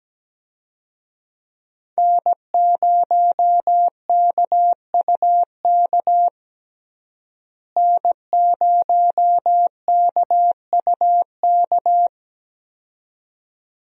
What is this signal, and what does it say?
Morse code, heard as a keyed tone, about 17 wpm, 700 Hz. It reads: N0KUK N0KUK
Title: USA Call Signs - 5 Characters - Repeated 2x - 17wpm